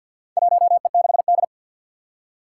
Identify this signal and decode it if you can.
Morse code, heard as a keyed tone, about 53 wpm, 700 Hz. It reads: 1E6D
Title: Four Challenging Letter-Number Combinations 50wpm